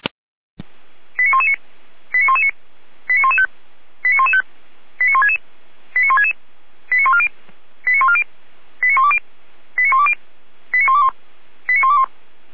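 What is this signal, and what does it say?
Five-tone selective call (ZVEI), heard as quick runs of five tones, tones of about 80 ms. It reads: 89119 89119 89115 89115 89166 89166 89140 89140 89120 89120 89121 89121
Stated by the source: ZVEI five-tone selective call received by SDR at fire brigade Orlamünde (feuerwehr-orlamuende).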